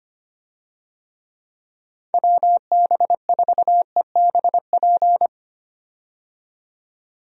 Morse code, heard as a keyed tone, about 25 wpm, 700 Hz. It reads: WB4EBP
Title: USA Call Signs - 6 Characters - 25wpm